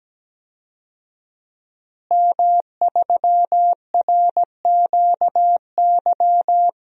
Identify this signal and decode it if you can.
Morse code, heard as a keyed tone, about 17 wpm, 700 Hz. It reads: M3RQY
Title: DXCC-100 Call Signs - 5 Characters - Repeated 2x - 17wpm